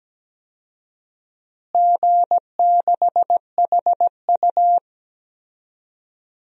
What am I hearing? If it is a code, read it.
Morse code, heard as a keyed tone, about 17 wpm, 700 Hz. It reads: G6HU
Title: DXCC-100 Call Signs - 4 Characters - Repeated 3x - 17wpm